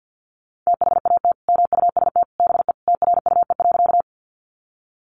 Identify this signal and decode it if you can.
Morse code, heard as a keyed tone, about 50 wpm, 700 Hz. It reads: THAT MUST BE TRUE.
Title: Sentences from Top 400 Words 50wpm